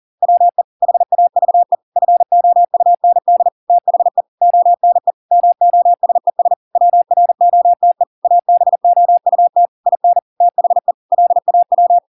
Morse code, heard as a keyed tone, about 40 wpm, 700 Hz. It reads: WE HAVE FOUND THE ONE MOSES WROTE ABOUT IN THE LAW